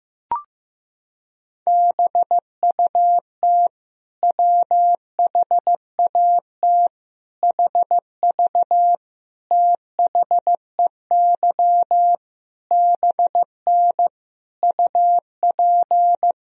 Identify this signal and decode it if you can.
Morse code, heard as a keyed tone, about 15 wpm, 700 Hz. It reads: BUT WHAT HV THEY BN UP
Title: Sentences from Top 100 Words and Abbreviations 15wpm